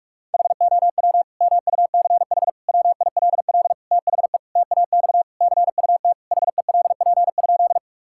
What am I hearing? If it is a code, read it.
Morse code, heard as a keyed tone, about 41 wpm, 700 Hz. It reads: HOW MUCH WILL THE TAX CUT HELP?
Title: Sentences from Top 1000 Words - Review Part 5 - 45wpm